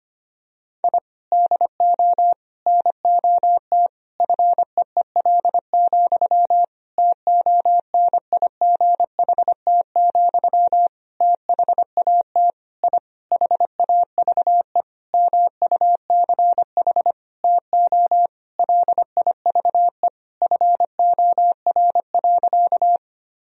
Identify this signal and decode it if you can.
Morse code, heard as a keyed tone, about 25 wpm, 700 Hz. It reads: I DO NOT FEEL, TONIGHT, THAT I HAVE MUCH TO LIVE FOR.
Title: Sentences from Top 1000 Words - Review Part 3 - 25wpm